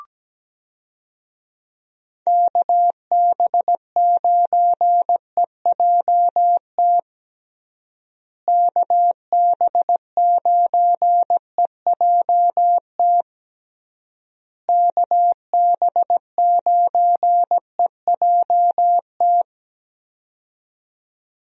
Morse code, heard as a keyed tone, about 17 wpm, 700 Hz. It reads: KB9EJT KB9EJT KB9EJT
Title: USA Call Signs - 6 Characters - Repeated 3x - 17wpm